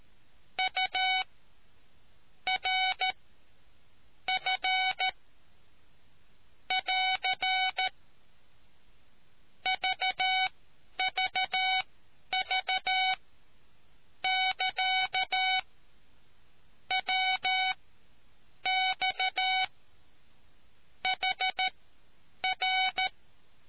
Morse code, heard as a keyed tone, about 14 wpm, spaced as for 8 wpm, 750 Hz. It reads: U R F <AR> VVV <KA> WX HR